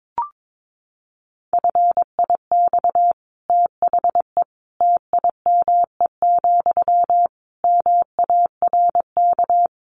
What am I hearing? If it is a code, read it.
Morse code, heard as a keyed tone, about 22 wpm, 700 Hz. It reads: FIX THE TIME, MARK
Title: Sentences from Top 1400 Words 22wpm